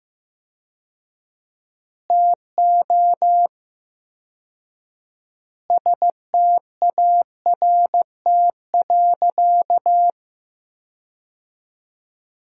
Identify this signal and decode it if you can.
Morse code, heard as a keyed tone, about 15 wpm, 700 Hz. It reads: TO START.
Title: 4x Word-Spacing - Sentences from Top 300 Words - 15wpm